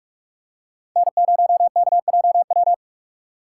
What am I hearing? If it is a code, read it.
Morse code, heard as a keyed tone, about 46 wpm, 700 Hz. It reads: N0KJW
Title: USA Call Signs - 5 Characters - 6x Character Spacing - 45wpm